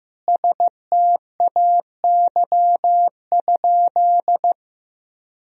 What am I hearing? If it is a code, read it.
Morse code, heard as a keyed tone, about 16 wpm, 700 Hz. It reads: STAY?
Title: Sentences from Top 1000 Words - Review Part 1 - 15wpm